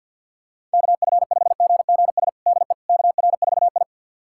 Morse code, heard as a keyed tone, about 48 wpm, 700 Hz. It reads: KF5GGS DE KR4I
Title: USA Callsign DE Callsign 50wpm